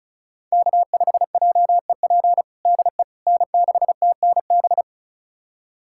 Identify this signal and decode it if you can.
Morse code, heard as a keyed tone, about 36 wpm, 700 Hz. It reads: K5JEP DE N6TNB
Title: USA Callsign DE Callsign - 5 Characters - 35wpm